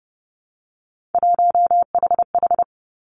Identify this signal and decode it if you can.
Morse code, heard as a keyed tone, about 30 wpm, 700 Hz. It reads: 1HH
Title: Three Challenging Letter-Number Combinations 30wpm